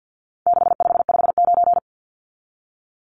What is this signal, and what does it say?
Morse code, heard as a keyed tone, about 50 wpm, 700 Hz. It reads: 6559
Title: Numbers - 4-Digit 50wpm